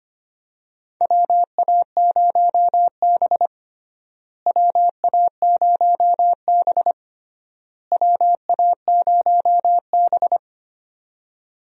Morse code, heard as a keyed tone, about 25 wpm, 700 Hz. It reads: WA0B WA0B WA0B
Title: USA Call Signs - 4 Characters - Repeated 3x - 25wpm